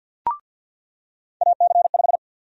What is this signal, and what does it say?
Morse code, heard as a keyed tone, about 48 wpm, 700 Hz. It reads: AC5
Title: USA Call Sign Prefixes - 50wpm